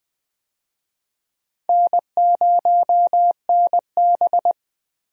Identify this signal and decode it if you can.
Morse code, heard as a keyed tone, about 21 wpm, 700 Hz. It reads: N0NB
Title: USA Call Signs - 4 Characters - Repeated 2x - 20wpm